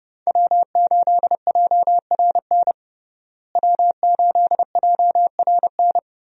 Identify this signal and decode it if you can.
Morse code, heard as a keyed tone, about 30 wpm, 700 Hz. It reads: W8JRN W8JRN
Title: USA Call Signs - 5 Characters - Repeated 3x - 30wpm